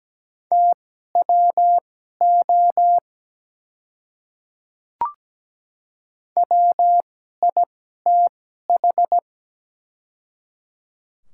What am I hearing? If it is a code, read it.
Morse code, heard as a keyed tone, about 17 wpm, spaced as for 12 wpm, 700 Hz. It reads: TWO WITH
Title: 2x Character Spacing - Top 100 Words - 17wpm